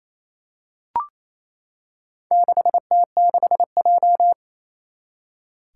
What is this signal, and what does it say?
Morse code, heard as a keyed tone, about 28 wpm, 700 Hz. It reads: 6T6J
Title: Four Challenging Letter-Number Combinations 28wpm